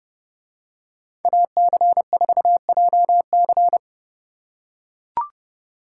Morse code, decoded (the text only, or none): AC4JC